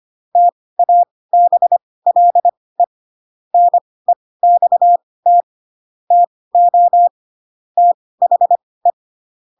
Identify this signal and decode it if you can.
Morse code, heard as a keyed tone, about 25 wpm, spaced as for 18 wpm, 700 Hz. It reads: TABLE NEXT TO THE
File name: Pinocchio / Pinocchio ch7 12 25 ik1hge